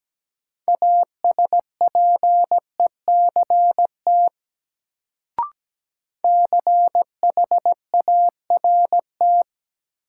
Morse code, heard as a keyed tone, about 17 wpm, 700 Hz. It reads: ASPECT CHART
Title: Sentences from Top 1300 Words - New Word Drill Alt - 17wpm